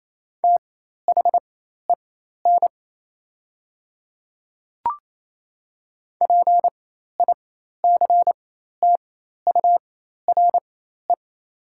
Morse code, heard as a keyed tone, about 27 wpm, spaced as for 13 wpm, 690 Hz. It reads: THEN PICTURE
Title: 4x Character Spacing - Top 200 Words - 28wpm